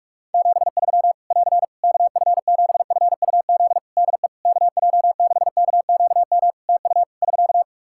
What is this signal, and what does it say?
Morse code, heard as a keyed tone, about 46 wpm, 700 Hz. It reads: 73 <AR> KF7FUZ DE KJ6KQM TU <SK>
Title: Practice QSO - Exchange 4 - Ending - 45wpm